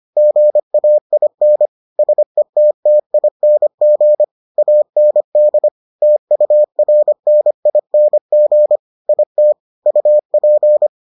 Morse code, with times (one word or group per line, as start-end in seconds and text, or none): 0.2-1.7 GAIN
2.0-4.2 SETTING
4.6-5.7 AND
6.0-8.8 TURNING
9.1-9.5 IT
9.9-10.9 UP